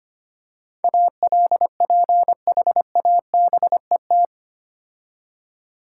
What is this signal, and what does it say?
Morse code, heard as a keyed tone, about 25 wpm, 690 Hz. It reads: ALPHABET
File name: Easy Words - Speed-Racing with Hints - 25wpm